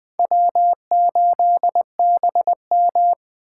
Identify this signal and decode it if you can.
Morse code, heard as a keyed tone, about 19 wpm, 700 Hz. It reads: W8BM